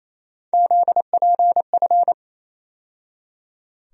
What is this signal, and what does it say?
Morse code, heard as a keyed tone, about 28 wpm, 700 Hz. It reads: ZPF
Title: USA Call Sign Postfixes - 3 Letters 28wpm